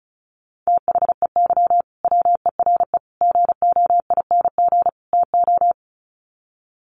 Morse code, heard as a keyed tone, about 34 wpm, 700 Hz. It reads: THEY WERE GOING TO
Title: Sets of 4 Words 35wpm